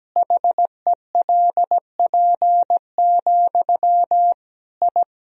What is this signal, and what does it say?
Morse code, heard as a keyed tone, about 17 wpm, 700 Hz. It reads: HELP, I
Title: Sentences from Top 1000 Words - Review Part 1 - 17wpm